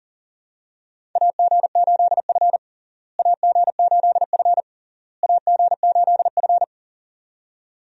Morse code, heard as a keyed tone, about 40 wpm, 700 Hz. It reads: AG8F AG8F AG8F
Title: USA Call Signs - 4 Characters - Repeated 3x - 40wpm